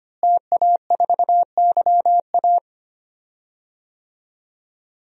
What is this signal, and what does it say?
Morse code, heard as a keyed tone, about 25 wpm, 700 Hz. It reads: TA4YA
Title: DXCC-100 Call Signs - 5 Characters - Repeated 3x - 25wpm